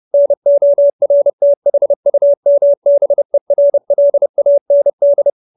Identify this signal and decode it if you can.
Morse code, heard as a keyed tone, about 30 wpm, 570 Hz. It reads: NORTHUMBERLAND